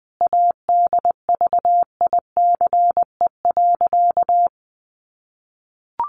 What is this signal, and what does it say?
Morse code, heard as a keyed tone, about 19 wpm, 700 Hz. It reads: ADVICE.
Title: Sentences from Top 1100 Words 20wpm